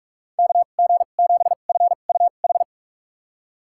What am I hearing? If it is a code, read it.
Morse code, heard as a keyed tone, about 41 wpm, spaced as for 31 wpm, 700 Hz. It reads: KG7FUH
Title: USA Call Signs - 6 Characters - 2x Character Spacing - 45wpm